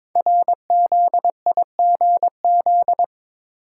Morse code, heard as a keyed tone, about 22 wpm, 700 Hz. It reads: RZIGZ
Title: Code Groups - 5 Letters - 22wpm